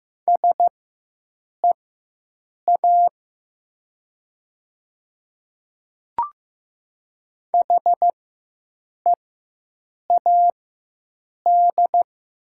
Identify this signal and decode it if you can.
Morse code, heard as a keyed tone, about 15 wpm, spaced as for 7 wpm, 700 Hz. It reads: SEA HEAD
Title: 4x Character Spacing - Top 500 Words - 15wpm